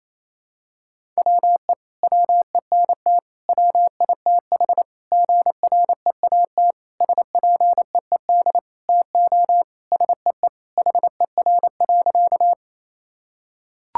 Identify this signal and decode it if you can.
Morse code, heard as a keyed tone, about 28 wpm, 700 Hz. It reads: WE WENT WITH GREAT SPEED TO SEE HER.
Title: Sentences from Top 1000 Words - Review Part 5 - 28wpm